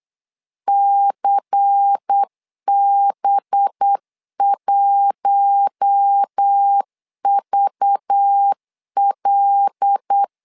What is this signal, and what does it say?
Morse code, heard as a keyed tone, about 8 wpm, 800 Hz. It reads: CB1VL
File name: LEZIONI / 5 LBVC / 5 LBVC 8wpm 2935